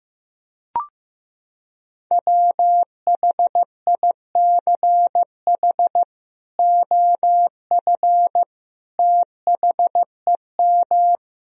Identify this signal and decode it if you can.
Morse code, heard as a keyed tone, about 16 wpm, 700 Hz. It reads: WHICH OF THEM
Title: Sentences from Top 1000 Words - Review Part 3 - 15wpm